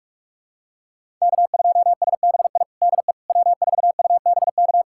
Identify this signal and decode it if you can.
Morse code, heard as a keyed tone, about 46 wpm, 700 Hz. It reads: K2SBI DE W4UBK
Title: USA Callsign DE Callsign - 5 Characters - 45wpm